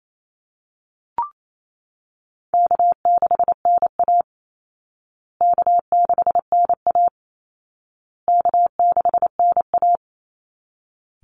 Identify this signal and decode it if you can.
Morse code, heard as a keyed tone, about 28 wpm, 700 Hz. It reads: K6NA K6NA K6NA
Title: Easy Call Signs Repeated 3x - 28wpm